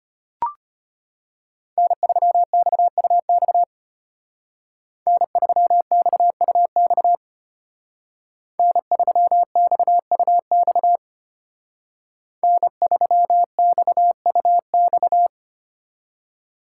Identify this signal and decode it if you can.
Morse code, heard as a keyed tone, about 31 wpm, 700 Hz. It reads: N3XUX N3XUX N3XUX N3XUX